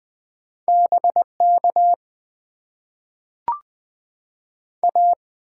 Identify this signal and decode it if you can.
Morse code, heard as a keyed tone, about 20 wpm, 700 Hz. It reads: BK A
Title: Top 90 QSO Elements - 8x Character Spacing - 20wpm